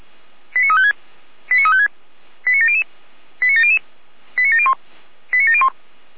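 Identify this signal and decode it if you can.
Five-tone selective call (ZVEI), heard as quick runs of five tones, tones of about 70 ms. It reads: 89356 89356 89700 89700 89701 89701
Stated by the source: ZVEI five-tone selective call received on an SDR.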